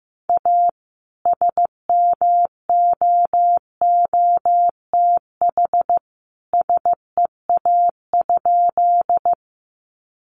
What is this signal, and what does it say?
Morse code, heard as a keyed tone, about 15 wpm, 700 Hz. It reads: A SMOOTH SEA?
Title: Sentences from Top 1400 Words 15wpm